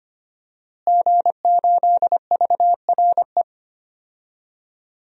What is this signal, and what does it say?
Morse code, heard as a keyed tone, about 25 wpm, 700 Hz. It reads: G8VRE
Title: DXCC-100 Call Signs - 5 Characters - Repeated 3x - 25wpm